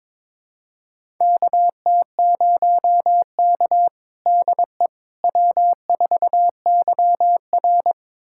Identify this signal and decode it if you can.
Morse code, heard as a keyed tone, about 22 wpm, 700 Hz. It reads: KT0K DE W4YR
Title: USA Callsign DE Callsign - 4 Characters - 22wpm